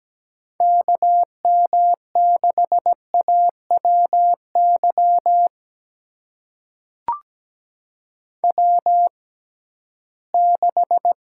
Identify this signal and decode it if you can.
Morse code, heard as a keyed tone, about 17 wpm, spaced as for 13 wpm, 700 Hz. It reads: KM6AWY W6
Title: USA Call Signs - 6x Character Spacing - 17wpm